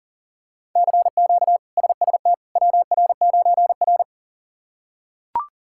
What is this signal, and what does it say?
Morse code, heard as a keyed tone, about 40 wpm, 700 Hz. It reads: CQ SST WR9R
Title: K1USN SST QSO Practice - 40wpm